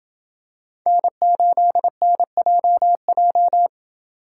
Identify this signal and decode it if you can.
Morse code, heard as a keyed tone, about 27 wpm, 700 Hz. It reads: N8NJJ